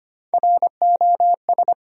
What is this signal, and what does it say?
Morse code, heard as a keyed tone, about 25 wpm, 700 Hz. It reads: ROS
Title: USA Common Names - Max 5 Letters - 25wpm